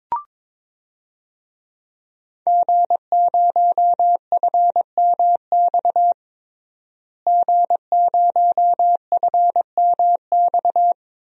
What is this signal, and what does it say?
Morse code, heard as a keyed tone, about 22 wpm, 700 Hz. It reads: G0FMX G0FMX
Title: DXCC-100 Call Signs - 5 Characters - Repeated 2x - 22wpm